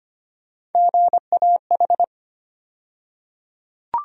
Morse code, heard as a keyed tone, about 25 wpm, 700 Hz. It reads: GAH
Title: USA Call Sign Postfixes - 3 Letters 25wpm